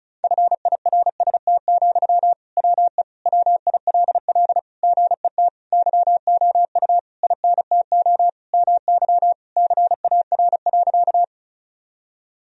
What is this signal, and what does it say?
Morse code, heard as a keyed tone, about 34 wpm, 700 Hz. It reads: FIRST, WE WILL GET YOU INTO MY CAR.
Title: Sentences from Top 1000 Words - Review Part 1 - 35wpm